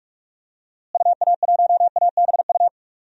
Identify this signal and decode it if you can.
Morse code, heard as a keyed tone, about 44 wpm, 700 Hz. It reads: UA1ABU
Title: DXCC-100 Call Signs - 6 Characters - Repeated 2x - 45wpm